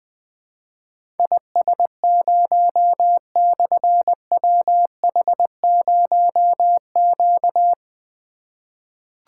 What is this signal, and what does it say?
Morse code, heard as a keyed tone, about 20 wpm, 700 Hz. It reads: IS0/WH0Q
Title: Difficult Call Signs 20wpm